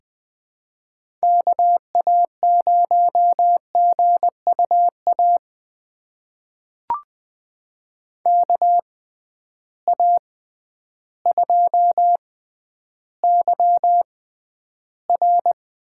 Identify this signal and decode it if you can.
Morse code, heard as a keyed tone, about 20 wpm, spaced as for 11 wpm, 700 Hz. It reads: KA0GUA KA2YR
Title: USA Call Signs - 6 Characters - 6x Character Spacing - 20wpm